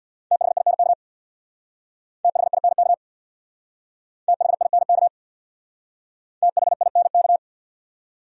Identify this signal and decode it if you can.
Morse code, heard as a keyed tone, about 63 wpm, 700 Hz. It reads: THINK THINK THINK THINK